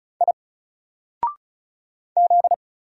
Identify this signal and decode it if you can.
Morse code, heard as a keyed tone, about 35 wpm, 700 Hz. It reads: I Z
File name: Single-Letter-Number-35wpm